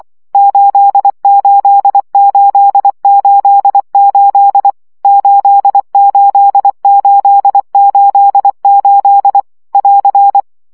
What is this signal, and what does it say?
Morse code, heard as a keyed tone, about 24 wpm, 800 Hz. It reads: 88888 88888 <AR>
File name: CW gyakorlas 8 20 baud